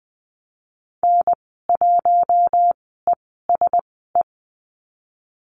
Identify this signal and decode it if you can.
Morse code, heard as a keyed tone, about 20 wpm, spaced as for 14 wpm, 700 Hz. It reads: N1ESE
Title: 042 Lesson 8g - Number 1 - Call Signs 20wpm